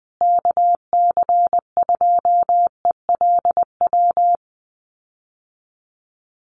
Morse code, heard as a keyed tone, about 20 wpm, 700 Hz. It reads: KC2ELW
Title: USA Call Signs Repeated 2x 20wpm